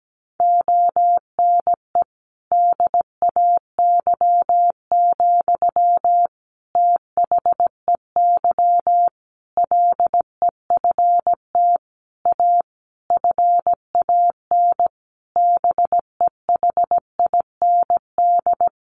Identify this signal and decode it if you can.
Morse code, heard as a keyed tone, about 17 wpm, 700 Hz. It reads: ONE DAY, THEY LEFT A FAN BEHIND
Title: Sentences from Top 1300 Words 17wpm